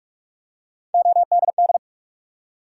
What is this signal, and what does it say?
Morse code, heard as a keyed tone, about 44 wpm, 700 Hz. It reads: ODD
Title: Three Letter Words - 45wpm